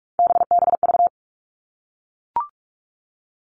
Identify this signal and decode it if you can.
Morse code, heard as a keyed tone, about 44 wpm, 700 Hz. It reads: BBV
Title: All Cut Numbers - 3-Digit 45wpm